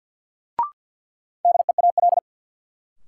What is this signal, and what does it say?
Morse code, heard as a keyed tone, about 51 wpm, 700 Hz. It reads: DEAL
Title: Sets of 3 Words - New Word Drill - 50wpm